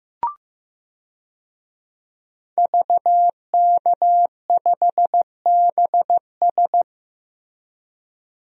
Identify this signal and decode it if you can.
Morse code, heard as a keyed tone, about 15 wpm, 700 Hz. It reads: VK5BS
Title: DXCC-100 Call Signs - 5 Characters - Repeated 2x - 15wpm